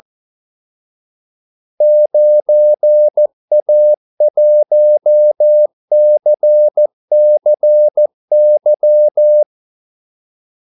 Morse code, heard as a keyed tone, about 14 wpm, 600 Hz. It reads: 9A1CCY
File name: Callsigns-14wpm